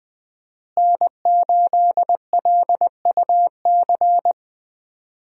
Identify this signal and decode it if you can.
Morse code, heard as a keyed tone, about 20 wpm, 700 Hz. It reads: N8LUC
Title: USA Call Signs - 5 Characters - Repeated 2x - 20wpm